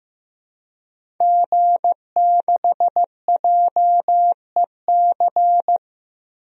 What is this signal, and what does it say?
Morse code, heard as a keyed tone, about 15 wpm, 700 Hz. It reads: G6JEC
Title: DXCC-100 Call Signs - 5 Characters - Repeated 2x - 15wpm